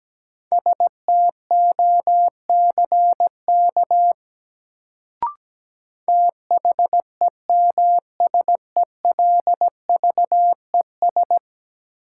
Morse code, heard as a keyed tone, about 17 wpm, 700 Hz. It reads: STOCK THEMSELVES
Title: Sentences from Top 800 Words - New Word Drill - 17wpm